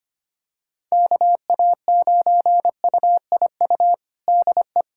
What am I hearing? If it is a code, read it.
Morse code, heard as a keyed tone, about 25 wpm, 700 Hz. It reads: KA9UIU DE
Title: USA Callsign DE Callsign - 6 Characters - 25wpm